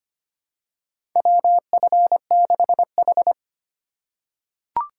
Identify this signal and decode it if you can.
Morse code, heard as a keyed tone, about 25 wpm, 700 Hz. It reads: WF6H
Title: USA Call Signs - 3x Character Spacing - 25wpm